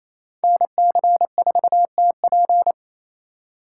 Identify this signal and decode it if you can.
Morse code, heard as a keyed tone, about 35 wpm, 700 Hz. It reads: NC4TP
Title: Speed Racing - USA 2x2 Call Signs 28wpm